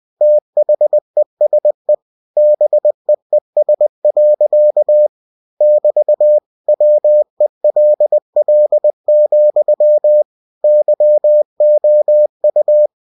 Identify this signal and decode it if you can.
Morse code, heard as a keyed tone, about 20 wpm, 600 Hz. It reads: THESE BEES. <BT> WELL, YOU